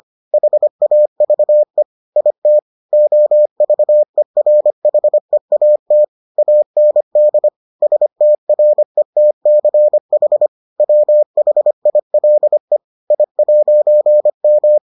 Morse code, heard as a keyed tone, about 25 wpm, 600 Hz. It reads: HAVE IT OVERHEAT AND STRETCH WHILE I'M